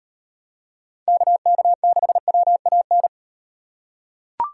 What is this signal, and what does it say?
Morse code, heard as a keyed tone, about 38 wpm, 700 Hz. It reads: KK6WAN